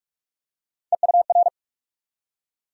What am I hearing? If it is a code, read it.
Morse code, heard as a keyed tone, about 46 wpm, 700 Hz. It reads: EUR
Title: USA Call Sign Postfixes - 3 Letters 45wpm